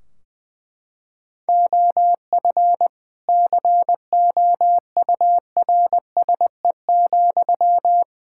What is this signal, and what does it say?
Morse code, heard as a keyed tone, about 20 wpm, 700 Hz. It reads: OF COURSE,
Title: Sentences from Top 1600 Words 20wpm